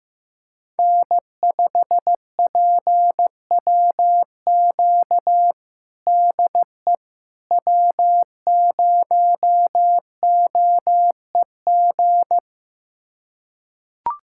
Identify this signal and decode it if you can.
Morse code, heard as a keyed tone, about 15 wpm, 700 Hz. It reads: N5PWQ DE W0OEG
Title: USA Callsign DE Callsign - 5 Characters - 15wpm